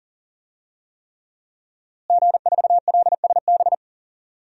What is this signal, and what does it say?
Morse code, heard as a keyed tone, about 40 wpm, 700 Hz. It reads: G4LSB
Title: DXCC-100 Call Signs - 5 Characters - Repeated 3x - 40wpm